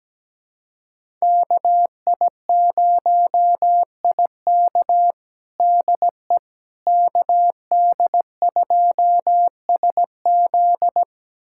Morse code, heard as a keyed tone, about 17 wpm, 700 Hz. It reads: KI0IK DE KD2SZ
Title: USA Callsign DE Callsign - 5 Characters - 17wpm